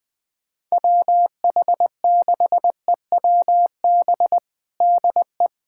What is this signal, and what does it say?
Morse code, heard as a keyed tone, about 21 wpm, 700 Hz. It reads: WH6EWB DE
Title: USA Callsign DE Callsign - 6 Characters - 20wpm